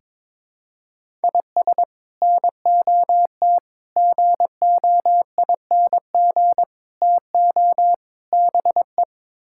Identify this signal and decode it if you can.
Morse code, heard as a keyed tone, about 21 wpm, 700 Hz. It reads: IS NOT GOING TO BE
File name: practice / 22wpm / Build A Phrase - 5-Word Phrases - 22wpm